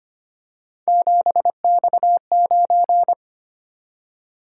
Morse code, heard as a keyed tone, about 25 wpm, 700 Hz. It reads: 7X9
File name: Three Letter-Number Combinations - 25wpm